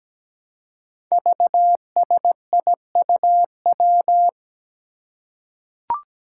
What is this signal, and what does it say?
Morse code, heard as a keyed tone, about 17 wpm, 700 Hz. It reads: VSIUW
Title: Five Challenging Letter-Number Combinations 17wpm